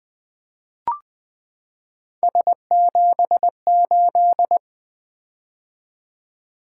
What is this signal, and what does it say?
Morse code, heard as a keyed tone, about 20 wpm, 700 Hz. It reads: S78
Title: Three Letter-Number Combinations - 20wpm